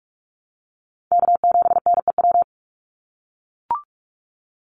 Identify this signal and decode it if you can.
Morse code, heard as a keyed tone, about 44 wpm, 700 Hz. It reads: K7NEW
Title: NA SOTA Chaser Callsigns 45wpm